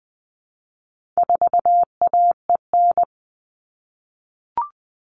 Morse code, heard as a keyed tone, about 20 wpm, 700 Hz. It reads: 4AEN
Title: All Cut Numbers - 4-Digit 20wpm